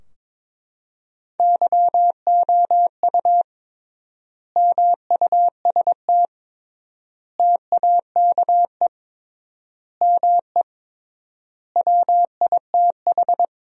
Morse code, heard as a keyed tone, about 22 wpm, 700 Hz. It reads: YOU MUST TAKE ME WITH
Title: 3x Word-Spacing - Sentences from Top 200 Words - 22wpm